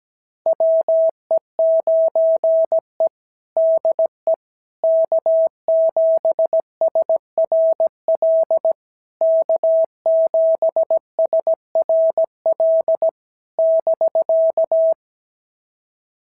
Morse code, heard as a keyed tone, about 16 wpm, 650 Hz. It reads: WE9E DE K7SRL K7SRL <BK>
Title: SOTA QSO Practice 17wpm